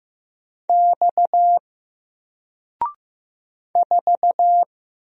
Morse code, heard as a keyed tone, about 15 wpm, 700 Hz. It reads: X 4
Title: Single Letter-Number Alt 15wpm